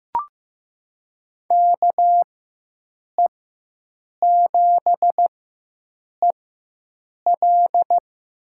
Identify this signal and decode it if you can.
Morse code, heard as a keyed tone, about 15 wpm, spaced as for 7 wpm, 700 Hz. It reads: KE7EL
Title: USA Call Signs - 5 Characters - 4x Character Spacing - 15wpm